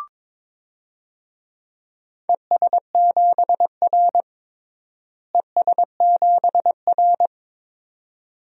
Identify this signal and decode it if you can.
Morse code, heard as a keyed tone, about 22 wpm, 700 Hz. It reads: ES7R ES7R
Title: DXCC-100 Call Signs - 4 Characters - Repeated 2x - 22wpm